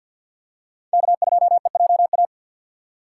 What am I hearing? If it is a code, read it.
Morse code, heard as a keyed tone, about 47 wpm, 700 Hz. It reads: K2EJA
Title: USA Call Signs - 5 Characters - 2x Character Spacing - 50wpm